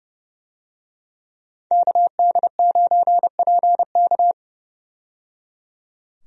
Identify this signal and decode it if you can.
Morse code, heard as a keyed tone, about 30 wpm, 700 Hz. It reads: KD9PK
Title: USA Call Signs - 5 Characters - 30wpm